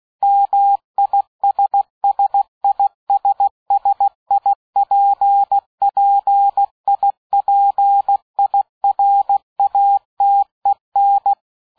Morse code, heard as a keyed tone, about 16 wpm, 800 Hz. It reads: MISSISSIPPIPIRATEN
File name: dk5ke / Fingerbrecher 75BpM